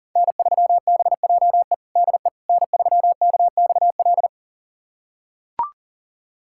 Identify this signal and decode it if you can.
Morse code, heard as a keyed tone, about 40 wpm, 700 Hz. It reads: N3BJE DE N3KXL
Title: USA Callsign DE Callsign - 5 Characters - 40wpm